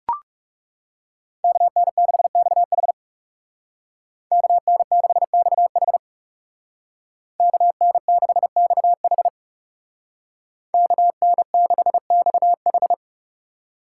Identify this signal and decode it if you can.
Morse code, heard as a keyed tone, about 40 wpm, 700 Hz. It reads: KN6XH KN6XH KN6XH KN6XH